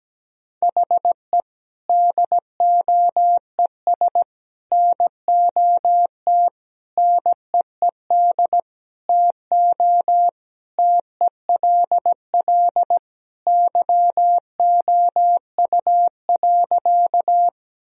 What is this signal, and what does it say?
Morse code, heard as a keyed tone, about 17 wpm, 700 Hz. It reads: HE DOES NOT NEED TO TELL YOU.
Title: Sentences from Top 1000 Words - Review Part 1 - 17wpm